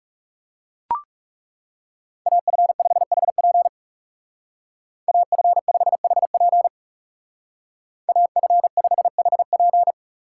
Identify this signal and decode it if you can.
Morse code, heard as a keyed tone, about 37 wpm, 700 Hz. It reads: AF5HP AF5HP AF5HP
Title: Speed Racing - USA Call Signs 30wpm